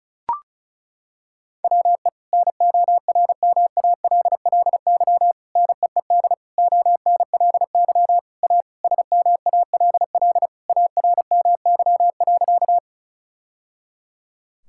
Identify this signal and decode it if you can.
Morse code, heard as a keyed tone, about 36 wpm, 690 Hz. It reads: WE NORMALLY NEED ONLY A SMALL ARMY.
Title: Sentences from Top 1000 Words - Review Part 6 - 35wpm